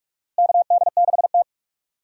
Morse code, heard as a keyed tone, about 46 wpm, 700 Hz. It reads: KD6T
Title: USA Call Signs - 4 Characters - Repeated 2x - 45wpm